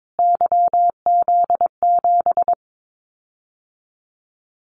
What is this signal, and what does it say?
Morse code, heard as a keyed tone, about 22 wpm, 700 Hz. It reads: YZ7
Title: Callsign Prefixes for Top 25 DXCC - Repeated 3x - 22wpm